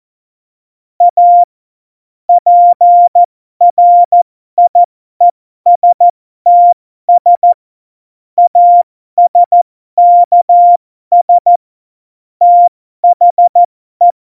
Morse code, heard as a keyed tone, about 14 wpm, spaced as for 12 wpm, 700 Hz. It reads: A PRIESTS ASKS THE
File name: Morsekurs / Witze